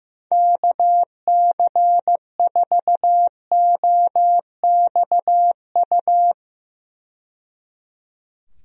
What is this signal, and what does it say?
Morse code, heard as a keyed tone, about 15 wpm, 700 Hz. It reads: KC4OXU
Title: USA Call Signs - 6 Characters - 15wpm